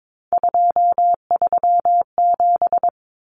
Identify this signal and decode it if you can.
Morse code, heard as a keyed tone, about 22 wpm, 700 Hz. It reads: 237
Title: Numbers - 3-Digit 22wpm